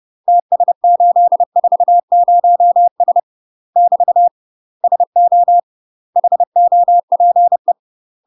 Morse code, heard as a keyed tone, about 30 wpm, 700 Hz. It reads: TS840S <BT> SO HOPE
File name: UK MASTER QSOs TRAINING / 30WPM / UK MASTER QSOs 0000